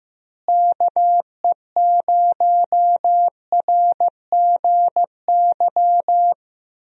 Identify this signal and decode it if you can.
Morse code, heard as a keyed tone, about 15 wpm, 700 Hz. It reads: KE0RGY
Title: USA Call Signs - 6 Characters - Repeated 2x - 15wpm